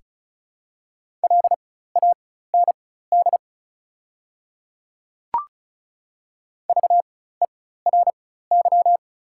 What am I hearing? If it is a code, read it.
Morse code, heard as a keyed tone, about 36 wpm, spaced as for 16 wpm, 700 Hz. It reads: LAND VERY
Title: 4x Character Spacing - Top 200 Words - 35wpm